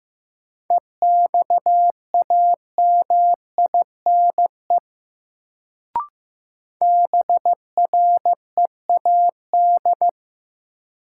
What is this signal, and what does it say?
Morse code, heard as a keyed tone, about 15 wpm, 700 Hz. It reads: EXAMINE BREAD
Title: Sentences from Top 1600 Words - New Word Drill - 15wpm